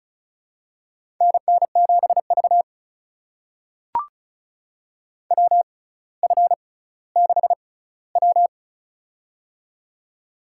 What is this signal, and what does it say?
Morse code, heard as a keyed tone, about 34 wpm, spaced as for 18 wpm, 700 Hz. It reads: NN7V WF6W
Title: USA Call Signs - 4 Characters - 6x Character Spacing - 35wpm